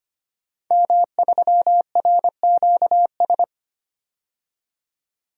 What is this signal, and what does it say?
Morse code, heard as a keyed tone, about 25 wpm, 700 Hz. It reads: M3RQS
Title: DXCC-100 Call Signs - 5 Characters - 25wpm